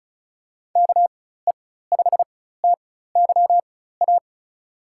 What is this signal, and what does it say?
Morse code, heard as a keyed tone, about 36 wpm, spaced as for 16 wpm, 700 Hz. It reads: KE5TYA
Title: USA Call Signs - 6 Characters - 4x Character Spacing - 35wpm